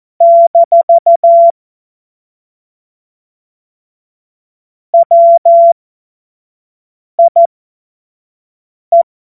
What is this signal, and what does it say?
Morse code, heard as a keyed tone, about 14 wpm, spaced as for 5 wpm, 680 Hz. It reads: - WIE